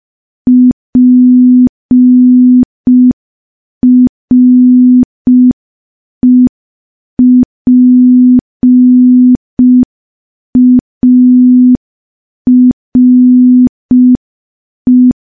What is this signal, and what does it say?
Morse code, heard as a keyed tone, about 5 wpm, 250 Hz. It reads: PREPARE
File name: morse / morse-training-05WPM